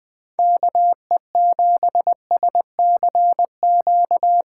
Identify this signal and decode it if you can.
Morse code, heard as a keyed tone, about 23 wpm, 700 Hz. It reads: KE7SCQ
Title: Speed Racing - USA 2x3 Call Signs 20wpm